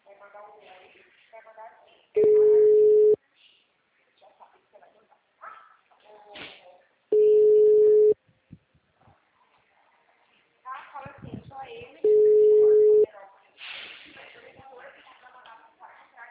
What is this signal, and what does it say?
Ringback tone (Europe, 425 Hz), with speech.